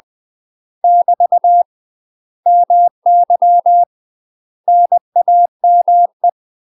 Morse code, heard as a keyed tone, about 20 wpm, 700 Hz. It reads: <BT> MY NAME
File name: UK MASTER QSOs TRAINING / 20WPM / UK MASTER QSOs 0024